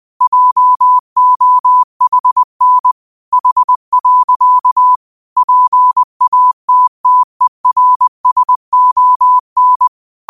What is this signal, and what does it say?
Morse code, heard as a keyed tone, about 20 wpm, 1000 Hz. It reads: JOHN H. PATTERSON
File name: qotdfiles / 20wpm